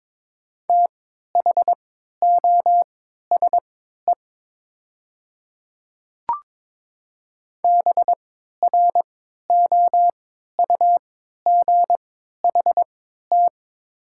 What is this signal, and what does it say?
Morse code, heard as a keyed tone, about 22 wpm, spaced as for 12 wpm, 700 Hz. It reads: THOSE BROUGHT
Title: 3x Character Spacing - Top 500 Words - 22wpm